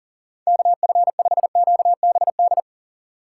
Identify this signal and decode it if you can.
Morse code, heard as a keyed tone, about 40 wpm, 700 Hz. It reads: KF5QBD
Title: USA Call Signs - 3x Character Spacing - 40wpm